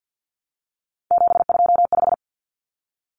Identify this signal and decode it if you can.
Morse code, heard as a keyed tone, about 50 wpm, 700 Hz. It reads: 725
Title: Common Cut Numbers - 3-Digit 50wpm